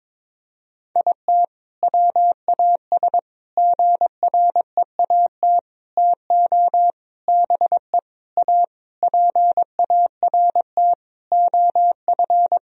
Morse code, heard as a keyed tone, about 22 wpm, 700 Hz. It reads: IT WAS GREAT TO BE A PART OF
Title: Sentences from Top 1000 Words - Review Part 2 - 22wpm